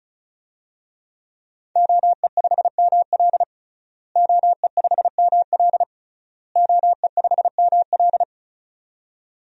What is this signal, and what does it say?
Morse code, heard as a keyed tone, about 34 wpm, 700 Hz. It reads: OE5ML OE5ML OE5ML
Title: DXCC-100 Call Signs - 5 Characters - Repeated 3x - 35wpm